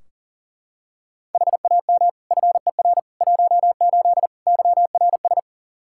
Morse code, heard as a keyed tone, about 40 wpm, 700 Hz. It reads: HAM FER 18 YRS
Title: Practice QSO - Exchange 3 - More Info - 40wpm